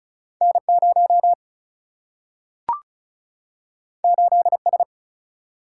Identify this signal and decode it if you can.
Morse code, heard as a keyed tone, about 36 wpm, 700 Hz. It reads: N 0 8 S
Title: Single-Letter Single-Number 35wpm